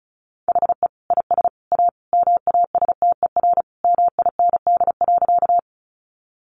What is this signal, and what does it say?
Morse code, heard as a keyed tone, about 34 wpm, 700 Hz. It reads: HE IS A MASTER MIND.
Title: Sentences from Top 1300 Words 35wpm